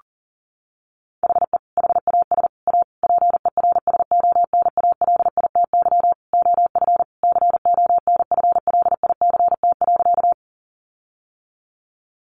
Morse code, heard as a keyed tone, about 40 wpm, 700 Hz. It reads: HE HAS A PERSONALITY OF CONFLICT.
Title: Sentences from Top 1400 Words 40wpm